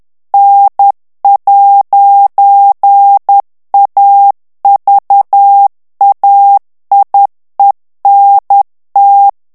Morse code, heard as a keyed tone, about 10 wpm, 800 Hz. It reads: N'AVAIENT
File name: FRANCAIS TEXTE CLAIR 2 VITESSE 10 MOTS MINUTES